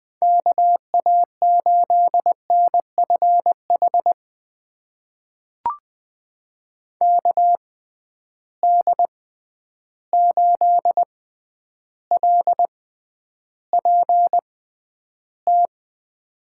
Morse code, heard as a keyed tone, about 20 wpm, spaced as for 10 wpm, 700 Hz. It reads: KA8NFH KD8LPT
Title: USA Call Signs - 6 Characters - 6x Character Spacing - 20wpm